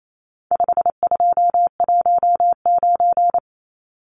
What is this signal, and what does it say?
Morse code, heard as a keyed tone, about 28 wpm, 700 Hz. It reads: 5219
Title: Numbers - 4-Digit 28wpm